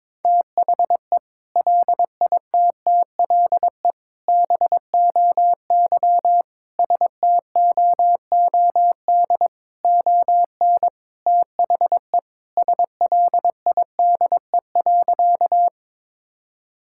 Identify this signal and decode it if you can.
Morse code, heard as a keyed tone, about 22 wpm, 700 Hz. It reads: THE LITTLE BOY STOOD ON THE SLIDE.
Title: Sentences from Top 1600 Words 22wpm